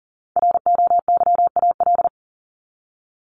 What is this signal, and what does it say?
Morse code, heard as a keyed tone, about 40 wpm, 700 Hz. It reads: ROYAL
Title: Top 1751-2000 Words 40wpm